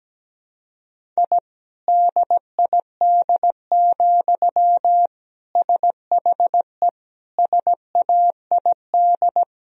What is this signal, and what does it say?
Morse code, heard as a keyed tone, about 17 wpm, 700 Hz. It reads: I DID, SHE SAID
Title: Sentences from Top 1000 Words - Review Part 4 - 17wpm